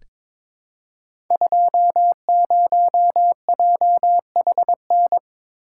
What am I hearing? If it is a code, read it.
Morse code, heard as a keyed tone, about 22 wpm, 700 Hz. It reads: 20JHN
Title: Code Groups - 5 Letter:Numbers - 22wpm